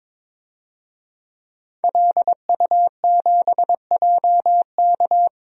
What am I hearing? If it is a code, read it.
Morse code, heard as a keyed tone, about 22 wpm, 700 Hz. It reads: LU7JK
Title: DXCC-100 Call Signs - 5 Characters - Repeated 2x - 22wpm